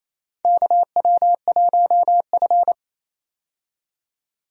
Morse code, heard as a keyed tone, about 28 wpm, 700 Hz. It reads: KW1F
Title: USA Call Signs - 4 Characters - 28wpm